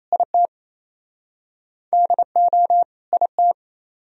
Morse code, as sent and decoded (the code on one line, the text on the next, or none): .. - / -.. --- / .. -
IT DO IT